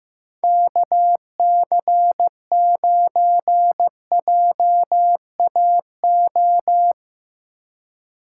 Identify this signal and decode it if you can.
Morse code, heard as a keyed tone, about 15 wpm, 700 Hz. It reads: KC9JAO
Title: USA Call Signs - 6 Characters - 2x Character Spacing - 15wpm